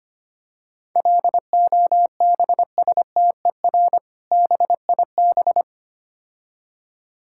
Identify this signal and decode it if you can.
Morse code, heard as a keyed tone, about 25 wpm, 700 Hz. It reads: LOBSTER BIB
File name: practice / 25wpm / Difficult Words - Speed-Racing with Hints - 25wpm